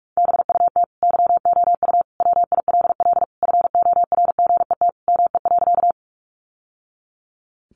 Morse code, heard as a keyed tone, about 44 wpm, 700 Hz. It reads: BUT YOU WILL FORGET ME.
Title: Sentences from Top 1200 Words 45wpm